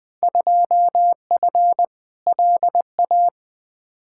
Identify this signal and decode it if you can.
Morse code, heard as a keyed tone, about 21 wpm, 700 Hz. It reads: 2F LA
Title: Field Day Exchanges - 20wpm